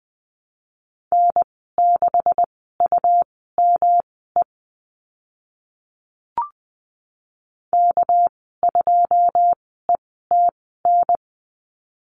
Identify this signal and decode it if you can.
Morse code, heard as a keyed tone, about 20 wpm, spaced as for 14 wpm, 700 Hz. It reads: N6UME K2ETN
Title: 213 Lesson 34g - Number 8 - Call Signs 20wpm